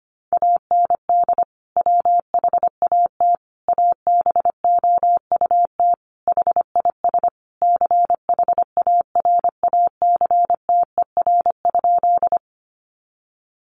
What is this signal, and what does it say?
Morse code, heard as a keyed tone, about 25 wpm, 700 Hz. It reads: AND WHAT ABOUT HIS CHARACTER?
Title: Sentences from Top 1200 Words 25wpm